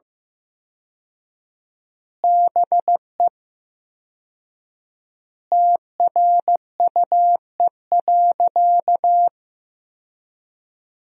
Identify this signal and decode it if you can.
Morse code, heard as a keyed tone, about 15 wpm, 700 Hz. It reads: BE TRUE.
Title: 4x Word-Spacing - Sentences from Top 400 Words - 15wpm